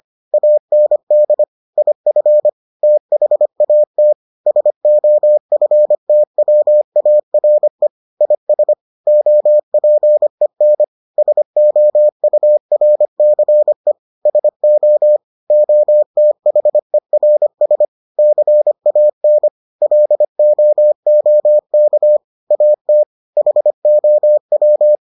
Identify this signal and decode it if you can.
Morse code, heard as a keyed tone, about 25 wpm, 600 Hz. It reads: AND IF THAT SOFTWARE IS OPEN SOURCE SO OTHERS CAN LOOK AT HOW